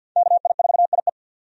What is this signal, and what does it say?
Morse code, heard as a keyed tone, about 47 wpm, 700 Hz. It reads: KI4IE
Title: USA Call Signs - 5 Characters - 50wpm